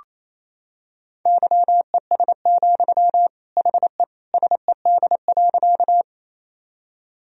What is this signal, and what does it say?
Morse code, heard as a keyed tone, about 28 wpm, 700 Hz. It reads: YES, HE SED.
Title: Sentences from Top 500 Words and Abbreviations 28wpm